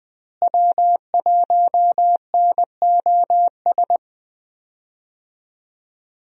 Morse code, heard as a keyed tone, about 20 wpm, 700 Hz. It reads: W1NOS
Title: USA Call Signs - 5 Characters - Repeated 2x - 20wpm